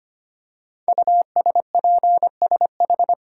Morse code, heard as a keyed tone, about 25 wpm, 700 Hz. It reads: USPSH
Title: Code Groups - 5 Letters - 25wpm